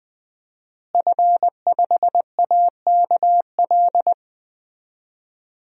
Morse code, heard as a keyed tone, about 20 wpm, 700 Hz. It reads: F5AKL
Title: DXCC-100 Call Signs - 5 Characters - Repeated 2x - 20wpm